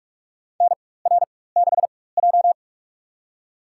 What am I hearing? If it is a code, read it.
Morse code, heard as a keyed tone, about 46 wpm, spaced as for 21 wpm, 700 Hz. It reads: NR6J
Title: USA Call Signs - 4 Characters - 4x Character Spacing - 45wpm